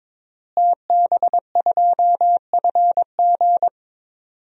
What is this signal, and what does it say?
Morse code, heard as a keyed tone, about 22 wpm, 700 Hz. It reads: TB2FG
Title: DXCC-100 Call Signs - 5 Characters - Repeated 3x - 22wpm